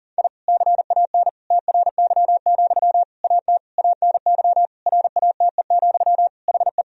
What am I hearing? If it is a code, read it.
Morse code, heard as a keyed tone, about 40 wpm, 700 Hz. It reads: I CAN TRY, AT ANY RATE, HE
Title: Sentences from Top 1000 Words - Review Part 4 - 40wpm